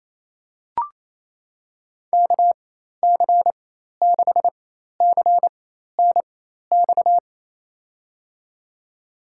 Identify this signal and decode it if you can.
Morse code, heard as a keyed tone, about 28 wpm, spaced as for 13 wpm, 700 Hz. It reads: KC6CNX
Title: USA Call Signs - 6 Characters - 4x Character Spacing - 28wpm